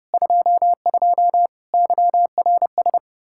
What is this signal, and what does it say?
Morse code, heard as a keyed tone, about 32 wpm, 700 Hz. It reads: 22 YRS
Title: Practice QSO - Exchange 3 - More Info - 30wpm